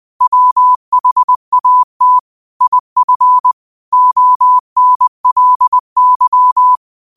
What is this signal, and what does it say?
Morse code, heard as a keyed tone, about 20 wpm, 1000 Hz. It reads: WHAT IF ONLY
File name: 20wpm